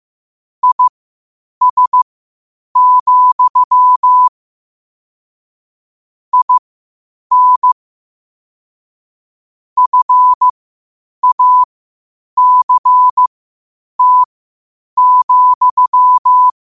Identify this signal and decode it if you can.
Morse code, heard as a keyed tone, about 15 wpm, spaced as for 8 wpm, 1000 Hz. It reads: IS, IN FACT,
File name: qotdfiles / 8wpm